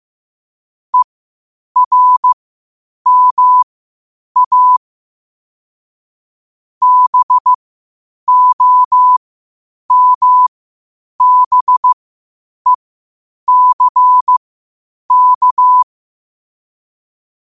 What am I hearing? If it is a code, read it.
Morse code, heard as a keyed tone, about 15 wpm, spaced as for 8 wpm, 1000 Hz. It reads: ERMA BOMBECK
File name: qotdfiles / 8wpm